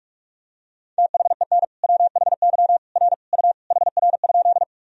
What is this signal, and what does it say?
Morse code, heard as a keyed tone, about 46 wpm, 700 Hz. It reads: THEN WHY R U HR?
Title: Sentences from Top 200 Words and Abbreviations 45wpm